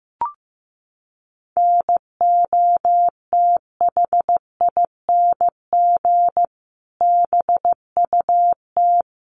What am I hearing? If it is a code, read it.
Morse code, heard as a keyed tone, about 15 wpm, 700 Hz. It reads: NOTHING BUT
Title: Sentences from Top 1300 Words 15wpm